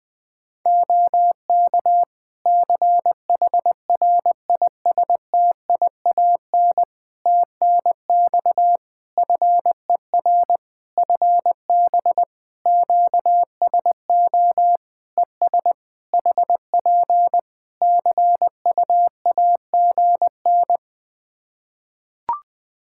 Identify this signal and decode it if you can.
Morse code, heard as a keyed tone, about 20 wpm, 700 Hz. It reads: OK CHRISTIAN TNX FER FB QSO ES HP CUAGN
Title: Practice QSO - Exchange 4 - Ending - 20wpm